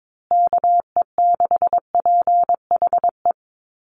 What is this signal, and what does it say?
Morse code, heard as a keyed tone, about 22 wpm, 700 Hz. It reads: KE6PHE
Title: USA Call Signs Repeated 3x 22wpm